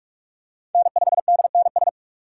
Morse code, heard as a keyed tone, about 48 wpm, 700 Hz. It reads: N5DNS
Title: USA Call Signs - 4x Character Spacing - 45wpm